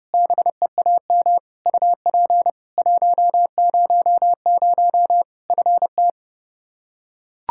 Morse code, heard as a keyed tone, about 32 wpm, 700 Hz. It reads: BEAM UP 100 FT
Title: Practice QSO - Exchange 2 - Info - 30wpm